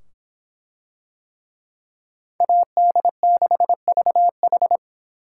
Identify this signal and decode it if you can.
Morse code, heard as a keyed tone, about 24 wpm, 700 Hz. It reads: AD6VH